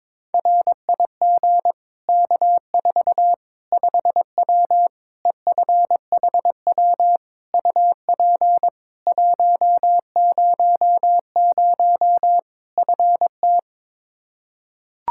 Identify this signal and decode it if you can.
Morse code, heard as a keyed tone, about 21 wpm, 700 Hz. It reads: RIG K4 5W EFHW UP 100 FT
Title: Practice QSO - Exchange 2 - Info - 22wpm